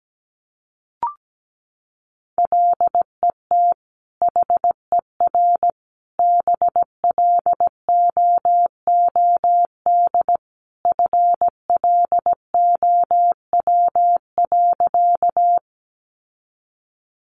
Morse code, heard as a keyed tone, about 17 wpm, 700 Hz. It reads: LET HER BLOOD FLOW.
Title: Sentences from Top 1200 Words 17wpm